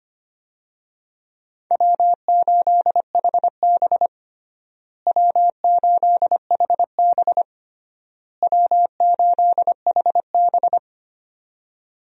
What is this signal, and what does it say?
Morse code, heard as a keyed tone, about 25 wpm, 700 Hz. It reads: W8HB W8HB W8HB
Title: USA Call Signs - 4 Characters - Repeated 3x - 25wpm